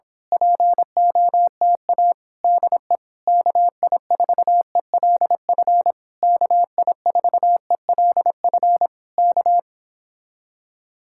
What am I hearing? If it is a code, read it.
Morse code, heard as a keyed tone, about 26 wpm, 690 Hz. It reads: POTA DE KI4ELF KI4ELF K
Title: Speed-Racing Common QSO Phrases - 22wpm